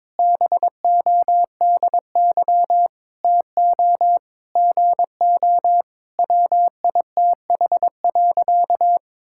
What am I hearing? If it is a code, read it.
Morse code, heard as a keyed tone, about 22 wpm, 700 Hz. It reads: BODY TO GO WITH.
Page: morse-code-ninja-practice-22wpm